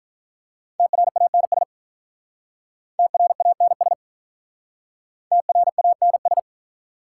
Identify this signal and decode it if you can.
Morse code, heard as a keyed tone, about 46 wpm, 700 Hz. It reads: TRANS TRANS TRANS